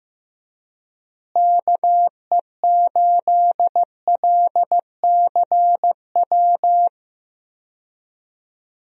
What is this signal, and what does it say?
Morse code, heard as a keyed tone, about 15 wpm, 700 Hz. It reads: KE8LCW
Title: USA Call Signs - 6 Characters - Repeated 2x - 15wpm